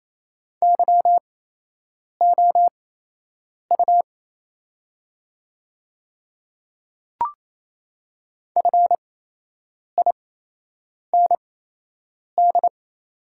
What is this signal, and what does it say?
Morse code, heard as a keyed tone, about 27 wpm, spaced as for 8 wpm, 690 Hz. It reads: YOU FIND
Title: 8x Character Spacing - Top 100 Words - 28wpm